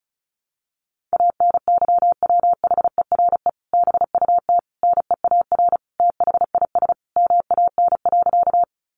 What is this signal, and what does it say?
Morse code, heard as a keyed tone, about 34 wpm, 700 Hz. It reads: ANYWHERE BUT NEAR THIS MAN.
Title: Sentences from Top 1400 Words 35wpm